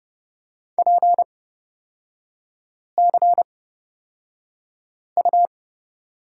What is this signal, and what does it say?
Morse code, heard as a keyed tone, about 30 wpm, spaced as for 5 wpm, 700 Hz. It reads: PCU
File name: Lesson 38e - Letter q - Three Characters 30wpm